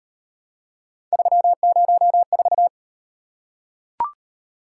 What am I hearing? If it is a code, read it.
Morse code, heard as a keyed tone, about 38 wpm, 700 Hz. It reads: 304